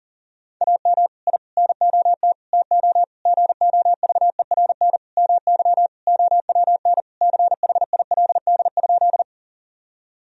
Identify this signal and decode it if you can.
Morse code, heard as a keyed tone, about 40 wpm, 700 Hz. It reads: AM I NOT TO GOVERN MY OWN CHILD?
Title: Sentences from Top 1000 Words - Review Part 2 - 40wpm